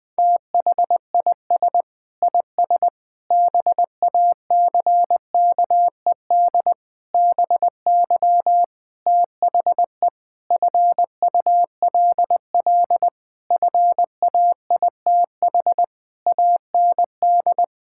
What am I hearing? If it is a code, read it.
Morse code, heard as a keyed tone, about 20 wpm, 700 Hz. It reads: THIS IS BACKED BY THE FULL FAITH AND
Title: Sentences with 9-Word Phrases - 20wpm